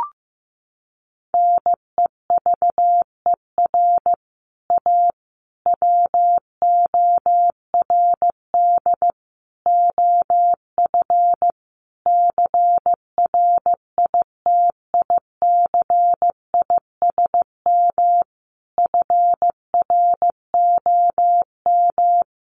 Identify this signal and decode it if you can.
Morse code, heard as a keyed tone, about 15 wpm, 700 Hz. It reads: NEVER A WORD OF CRITICISM FROM
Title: Sentences from Top 1300 Words 15wpm